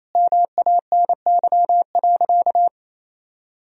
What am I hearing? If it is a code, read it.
Morse code, heard as a keyed tone, about 28 wpm, 700 Hz. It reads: MANY.
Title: Sentences from Top 1000 Words - Review Part 1 - 28wpm